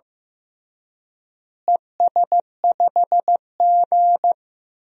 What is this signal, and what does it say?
Morse code, heard as a keyed tone, about 15 wpm, 700 Hz. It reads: ES5G
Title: DXCC-100 Call Signs - 4 Characters - Repeated 3x - 15wpm